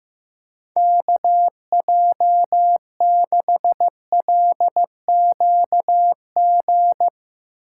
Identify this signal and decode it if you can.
Morse code, heard as a keyed tone, about 15 wpm, 700 Hz. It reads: KJ6LQG
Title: USA Call Signs - 4x Character Spacing - 15wpm